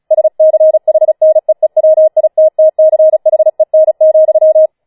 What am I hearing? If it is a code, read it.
Morse code, heard as a keyed tone, about 36 wpm, 630 Hz. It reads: SCHNEEWITTCHEN,
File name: Schneewittchen Teil6 36wpm